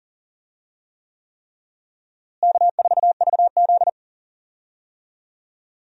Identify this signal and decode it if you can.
Morse code, heard as a keyed tone, about 40 wpm, 700 Hz. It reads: K4VZ
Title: USA Call Signs - 4 Characters - Repeated 3x - 40wpm